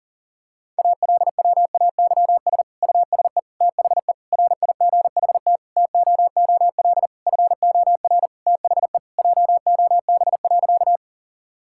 Morse code, heard as a keyed tone, about 37 wpm, 700 Hz. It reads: ALWAYS USE THE RIGHT TOOL FOR THE JOB.
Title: Sentences from Top 1000 Words - Review Part 6 - 40wpm